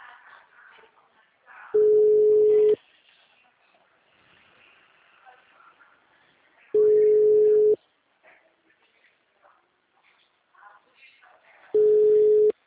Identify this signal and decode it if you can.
Ringback tone (Europe, 425 Hz).
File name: out- -2018- - - .11460